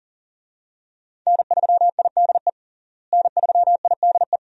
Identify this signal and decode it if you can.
Morse code, heard as a keyed tone, about 37 wpm, 700 Hz. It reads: N3IDE N3IDE
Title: USA Call Signs - 5 Characters - Repeated 2x - 40wpm